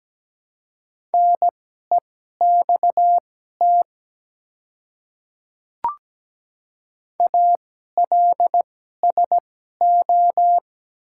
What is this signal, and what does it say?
Morse code, heard as a keyed tone, about 17 wpm, spaced as for 12 wpm, 700 Hz. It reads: NEXT ALSO
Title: 2x Character Spacing - Top 300 Words - 17wpm